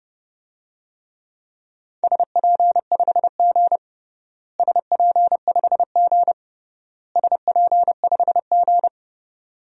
Morse code, heard as a keyed tone, about 30 wpm, 700 Hz. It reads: SP5G SP5G SP5G
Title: DXCC-100 Call Signs - 4 Characters - Repeated 3x - 30wpm